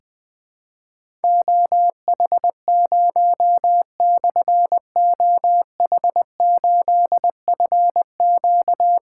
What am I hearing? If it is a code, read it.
Morse code, heard as a keyed tone, about 20 wpm, 700 Hz. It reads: OH0/OH8FQ
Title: Difficult Call Signs 20wpm